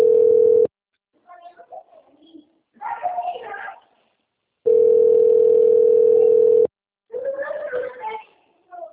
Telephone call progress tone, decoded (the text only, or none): ringback tone